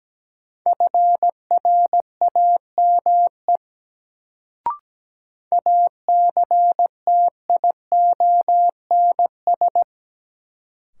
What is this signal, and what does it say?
Morse code, heard as a keyed tone, about 17 wpm, 700 Hz. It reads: FRAME ACTIONS
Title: Sentences from Top 1000 Words - Review Part 6 - New Word Drill - 17wpm